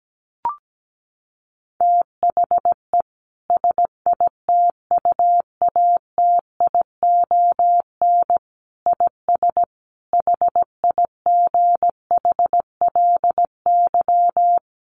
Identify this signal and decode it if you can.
Morse code, heard as a keyed tone, about 17 wpm, 700 Hz. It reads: THE SITUATION IS HIGHLY
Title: Sentences from Top 1300 Words 17wpm